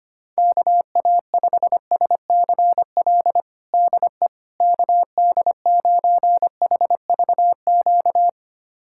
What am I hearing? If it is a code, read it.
Morse code, heard as a keyed tone, about 25 wpm, 700 Hz. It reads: KA5SCL DE KD9HVQ
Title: USA Callsign DE Callsign - 6 Characters - 25wpm